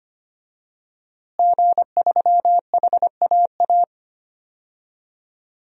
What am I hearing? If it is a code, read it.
Morse code, heard as a keyed tone, about 25 wpm, 700 Hz. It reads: G3HAA
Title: DXCC-100 Call Signs - 5 Characters - Repeated 3x - 25wpm